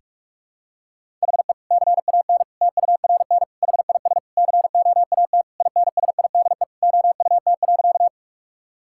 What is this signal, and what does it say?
Morse code, heard as a keyed tone, about 46 wpm, 700 Hz. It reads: HE CAN TURN HIS COAT INSIDE OUT.
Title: Sentences from Top 1600 Words 45wpm